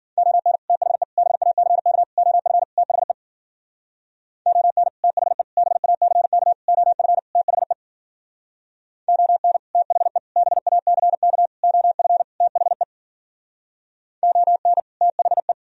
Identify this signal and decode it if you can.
Morse code, heard as a keyed tone, about 52 wpm, 700 Hz. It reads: ON THE BACK OF THE ON THE BACK OF THE ON THE BACK OF THE ON THE